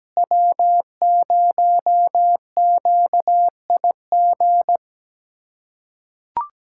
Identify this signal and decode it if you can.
Morse code, heard as a keyed tone, about 17 wpm, 700 Hz. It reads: W0QIG
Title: USA Callsign DE Callsign - 5 Characters - 17wpm